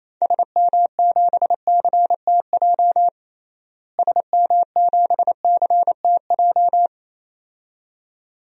Morse code, heard as a keyed tone, about 28 wpm, 700 Hz. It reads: SM7CTJ SM7CTJ
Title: DXCC-100 Call Signs - 6 Characters - Repeated 2x - 28wpm